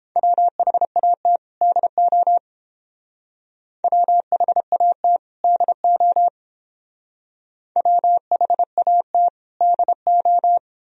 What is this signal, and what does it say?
Morse code, heard as a keyed tone, about 30 wpm, 700 Hz. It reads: WHAT DO WHAT DO WHAT DO